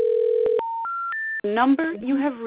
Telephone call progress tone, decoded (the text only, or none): special information tone